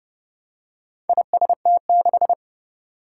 Morse code, heard as a keyed tone, about 30 wpm, 700 Hz. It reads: IST6
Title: Four Challenging Letter-Number Combinations 30wpm